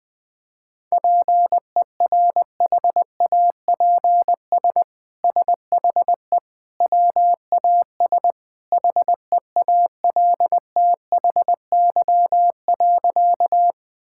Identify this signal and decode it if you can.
Morse code, heard as a keyed tone, about 20 wpm, 700 Hz. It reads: PERHAPS SHE WAS HEALTHY.
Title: Sentences from Top 1000 Words - Review Part 5 - 20wpm